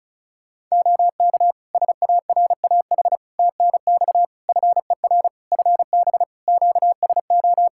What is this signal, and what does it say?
Morse code, heard as a keyed tone, about 35 wpm, 700 Hz. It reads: OK SARAH TNX FER FB QSO
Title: Practice QSO - Exchange 4 - Ending - 35wpm